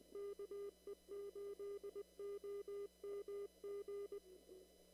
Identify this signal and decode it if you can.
Morse code, heard as a keyed tone, about 19 wpm, 400 Hz. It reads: KE8OMG
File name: CW / kt5j wpx cw-op-k5pi